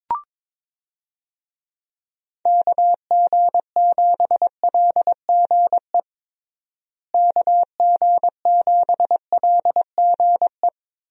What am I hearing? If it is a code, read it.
Morse code, heard as a keyed tone, about 22 wpm, 700 Hz. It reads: KG7LGE KG7LGE
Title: USA Call Signs - 6 Characters - Repeated 3x - 22wpm